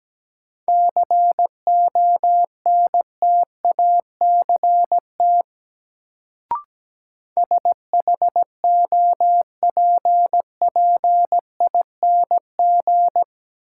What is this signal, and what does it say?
Morse code, heard as a keyed tone, about 17 wpm, 700 Hz. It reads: CONTACT SHOPPING
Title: Sentences from Top 1000 Words - Review Part 7 - New Word Drill - 17wpm